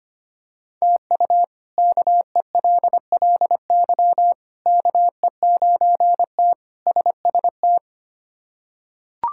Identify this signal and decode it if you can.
Morse code, heard as a keyed tone, about 25 wpm, 700 Hz. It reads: TU KELLY KE9T SST
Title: K1USN SST QSO Practice - 25wpm